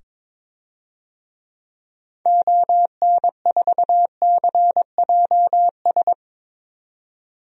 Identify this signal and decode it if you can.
Morse code, heard as a keyed tone, about 22 wpm, 700 Hz. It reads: ON4CJS
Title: DXCC-100 Call Signs - 6 Characters - 22wpm